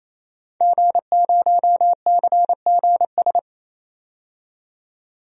Morse code, heard as a keyed tone, about 28 wpm, 700 Hz. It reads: G0CGS
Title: DXCC-100 Call Signs - 5 Characters - 28wpm